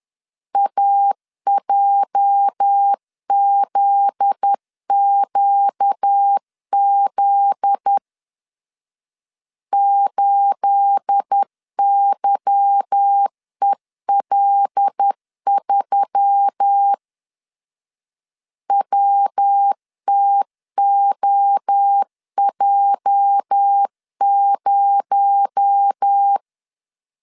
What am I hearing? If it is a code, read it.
Morse code, heard as a keyed tone, about 10 wpm, 800 Hz. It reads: AJZQZ 8YEL3 WTOJ0
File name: CW / DET / 1 9 1500C S B 10wpm 4439